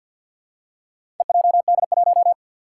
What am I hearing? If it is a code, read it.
Morse code, heard as a keyed tone, about 49 wpm, 700 Hz. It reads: EJD1
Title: Four Challenging Letter-Number Combinations 50wpm